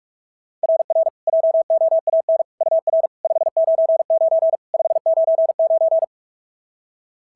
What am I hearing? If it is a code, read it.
Morse code, heard as a keyed tone, about 46 wpm, 690 Hz. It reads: RR JOAN UR 599 599
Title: SOTA QSO Practice 45wpm